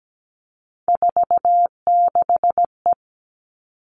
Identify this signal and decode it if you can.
Morse code, heard as a keyed tone, about 17 wpm, 700 Hz. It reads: 46E
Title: All Cut Numbers - 3-Digit 17wpm